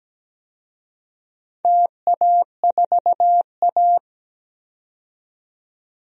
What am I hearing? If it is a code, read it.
Morse code, heard as a keyed tone, about 17 wpm, 700 Hz. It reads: TA4A
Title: DXCC-100 Call Signs - 4 Characters - 17wpm